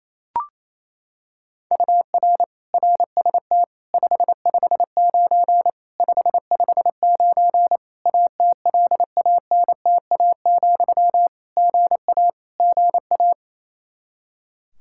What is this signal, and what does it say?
Morse code, heard as a keyed tone, about 28 wpm, 700 Hz. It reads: UR RST 559 559 ATLANTA, GA GA
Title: K1USN SST QSO Practice 28wpm